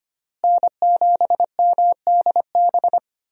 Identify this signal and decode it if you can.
Morse code, heard as a keyed tone, about 25 wpm, 700 Hz. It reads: N7MDB
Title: USA Call Signs - 5 Characters - Repeated 3x - 25wpm